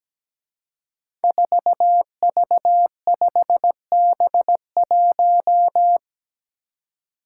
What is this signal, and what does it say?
Morse code, heard as a keyed tone, about 17 wpm, 700 Hz. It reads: 4V5B1
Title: Five Challenging Letter-Number Combinations 17wpm